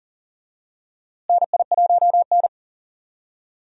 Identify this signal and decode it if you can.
Morse code, heard as a keyed tone, about 40 wpm, 700 Hz. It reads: NI1N
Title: USA Call Signs - 4 Characters - Repeated 3x - 40wpm